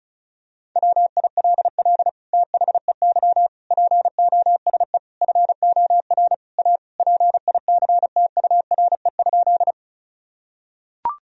Morse code, heard as a keyed tone, about 34 wpm, 700 Hz. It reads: WILL THEY POSE FOR A PICTURE?
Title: Sentences from Top 1000 Words - Review Part 2 - 35wpm